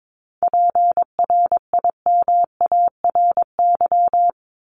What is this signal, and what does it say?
Morse code, heard as a keyed tone, about 22 wpm, 700 Hz. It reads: PRIMARY
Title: LY Suffix - Top 100 Words with Base Words 22wpm